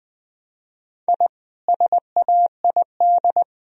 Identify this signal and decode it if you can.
Morse code, heard as a keyed tone, about 19 wpm, 700 Hz. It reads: I SAID